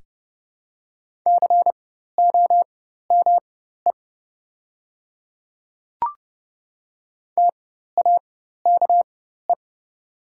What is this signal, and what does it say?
Morse code, heard as a keyed tone, about 30 wpm, spaced as for 14 wpm, 700 Hz. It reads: COME TAKE
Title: 4x Character Spacing - Top 300 Words - 30wpm